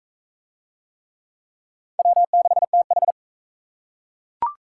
Morse code, heard as a keyed tone, about 40 wpm, 700 Hz. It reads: W6TH